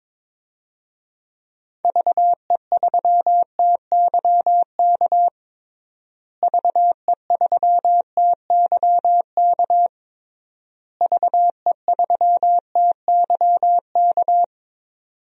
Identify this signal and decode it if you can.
Morse code, heard as a keyed tone, about 22 wpm, 700 Hz. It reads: VE3TYK VE3TYK VE3TYK
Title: DXCC-100 Call Signs - 6 Characters - Repeated 3x - 22wpm